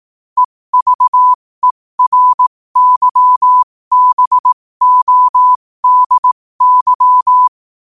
Morse code, heard as a keyed tone, about 18 wpm, spaced as for 15 wpm, 1000 Hz. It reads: EVERYBODY
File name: qotdfiles / 15wpm